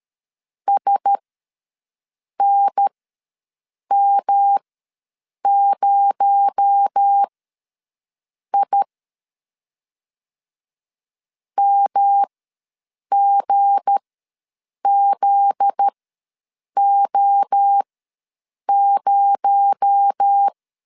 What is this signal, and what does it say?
Morse code, heard as a keyed tone, about 13 wpm, spaced as for 6 wpm, 800 Hz. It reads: SNM0I MGZO0
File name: LEZIONI / 7 GF9Z / 7 GF9Z ws 12wpm 3321